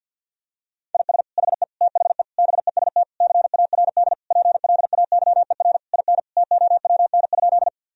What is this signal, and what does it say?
Morse code, heard as a keyed tone, about 50 wpm, 700 Hz. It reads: IS HE THE BEST CARD PLAYER IN TOWN?
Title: Sentences from Top 1000 Words - Review Part 4 - 50wpm